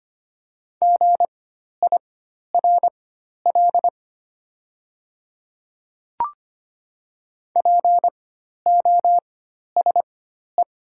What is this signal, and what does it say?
Morse code, heard as a keyed tone, about 25 wpm, spaced as for 12 wpm, 690 Hz. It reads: GIRL POSE
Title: 4x Character Spacing - Top 500 Words - 25wpm